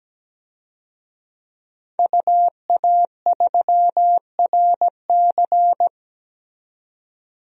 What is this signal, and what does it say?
Morse code, heard as a keyed tone, about 17 wpm, 700 Hz. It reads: UA3RC
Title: DXCC-100 Call Signs - 5 Characters - 17wpm